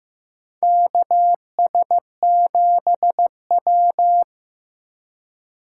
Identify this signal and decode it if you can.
Morse code, heard as a keyed tone, about 15 wpm, 700 Hz. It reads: KS7W
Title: USA Call Signs - 4 Characters - 3x Character Spacing - 15wpm